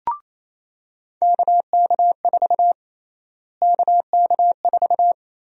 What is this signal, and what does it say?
Morse code, heard as a keyed tone, about 27 wpm, 700 Hz. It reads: KK4 KK4
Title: USA Call Sign Prefixes - Repeated 2x - 28wpm